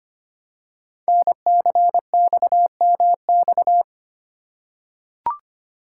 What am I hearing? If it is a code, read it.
Morse code, heard as a keyed tone, about 25 wpm, 700 Hz. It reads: NCXMX
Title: Code Groups - 5 Letters - 25wpm